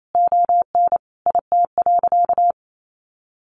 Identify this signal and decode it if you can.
Morse code, heard as a keyed tone, about 28 wpm, 700 Hz. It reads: ON IT.
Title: Sentences from Top 1300 Words 28wpm